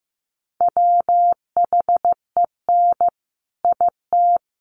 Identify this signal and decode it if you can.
Morse code, heard as a keyed tone, about 15 wpm, 700 Hz. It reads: WHEN IT
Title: Sets of 3 Words 15wpm